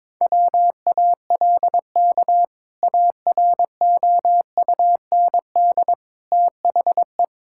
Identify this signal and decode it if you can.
Morse code, heard as a keyed tone, about 22 wpm, 700 Hz. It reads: WALK AROUND THE
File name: practice / 22wpm / Sentences with 2-Word Phrases - 22wpm